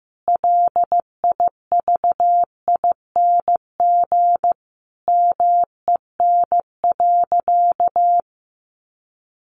Morse code, heard as a keyed tone, about 15 wpm, 700 Hz. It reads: LIVING MEN.
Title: Sentences from Top 1300 Words 15wpm